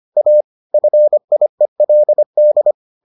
Morse code, heard as a keyed tone, about 25 wpm, 600 Hz. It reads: A FIELD